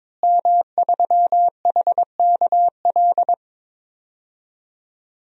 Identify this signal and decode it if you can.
Morse code, heard as a keyed tone, about 22 wpm, 700 Hz. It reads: M3HKL
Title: DXCC-100 Call Signs - 5 Characters - Repeated 2x - 22wpm